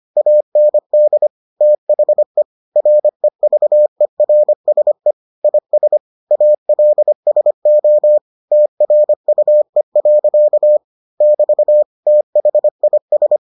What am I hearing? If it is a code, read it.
Morse code, heard as a keyed tone, about 25 wpm, 600 Hz. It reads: AND THE REVERSE IS ALSO TRUE. <BT> THIS